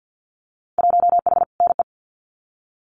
Morse code, heard as a keyed tone, about 50 wpm, 700 Hz. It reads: 1H NE
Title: Winter Field Day Exchanges 50wpm